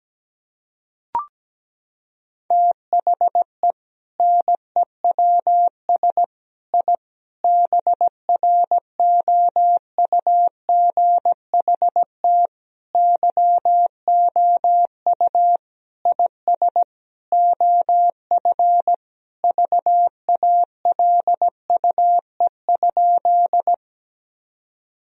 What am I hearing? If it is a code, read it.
Morse code, heard as a keyed tone, about 17 wpm, 700 Hz. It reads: THE NEWS I BROUGHT YOU IS OF VALUE?
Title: Sentences from Top 1000 Words - Review Part 3 - 17wpm